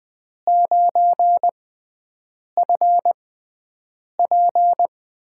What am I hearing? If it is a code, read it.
Morse code, heard as a keyed tone, about 20 wpm, spaced as for 7 wpm, 700 Hz. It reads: 9FP
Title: USA Call Signs - 5 Characters - 6x Character Spacing - 20wpm